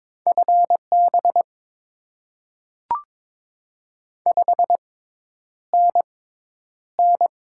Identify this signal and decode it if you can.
Morse code, heard as a keyed tone, about 22 wpm, spaced as for 10 wpm, 700 Hz. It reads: FB 5NN
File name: Top 20 QSO Elements - 6x Character Spacing - 22wpm